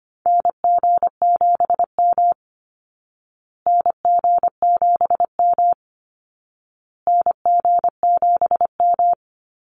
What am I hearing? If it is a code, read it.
Morse code, heard as a keyed tone, about 25 wpm, 700 Hz. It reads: NG7M NG7M NG7M
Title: Call Signs Repeated 2x - 25wpm